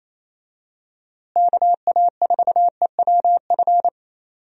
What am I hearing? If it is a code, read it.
Morse code, heard as a keyed tone, about 28 wpm, 700 Hz. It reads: KA4EWF
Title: USA Call Signs - 6 Characters - Repeated 3x - 28wpm